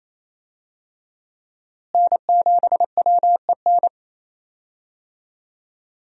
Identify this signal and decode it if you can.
Morse code, heard as a keyed tone, about 27 wpm, 700 Hz. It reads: N7WEN
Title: USA Call Signs - 5 Characters - Repeated 3x - 28wpm